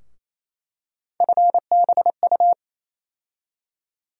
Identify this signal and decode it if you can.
Morse code, heard as a keyed tone, about 28 wpm, 700 Hz. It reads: FBU
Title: USA Call Sign Postfixes - 3 Letters 28wpm